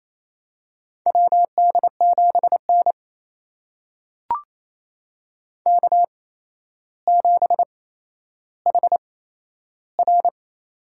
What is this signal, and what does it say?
Morse code, heard as a keyed tone, about 28 wpm, spaced as for 12 wpm, 700 Hz. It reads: WD7N K7HR
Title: USA Call Signs - 4 Characters - 8x Character Spacing - 28wpm